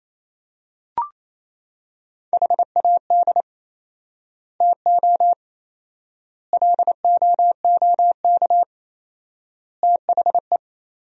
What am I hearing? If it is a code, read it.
Morse code, heard as a keyed tone, about 28 wpm, 700 Hz. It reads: HAD TO LOOK THE
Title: 4x Word-Spacing - Sentences from Top 100 Words - 28wpm